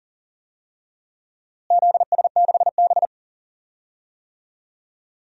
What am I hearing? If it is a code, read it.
Morse code, heard as a keyed tone, about 40 wpm, 700 Hz. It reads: ZS6B
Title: DXCC-100 Call Signs - 4 Characters - Repeated 3x - 40wpm